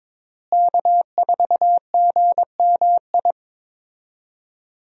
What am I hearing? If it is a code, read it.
Morse code, heard as a keyed tone, about 22 wpm, 700 Hz. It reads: K4GMI
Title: USA Call Signs - 5 Characters - Repeated 2x - 22wpm